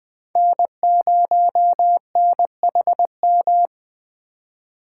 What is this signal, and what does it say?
Morse code, heard as a keyed tone, about 21 wpm, 700 Hz. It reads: N0NHM
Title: Speed Racing - USA 1x3 Call Signs 17wpm